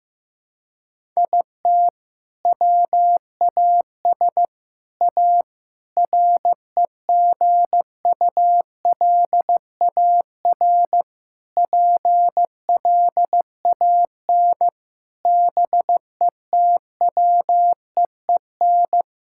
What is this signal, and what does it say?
Morse code, heard as a keyed tone, about 15 wpm, 700 Hz. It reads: IT WAS A REGULAR PLAN BETWEEN
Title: Sentences from Top 1000 Words - Review Part 6 - 15wpm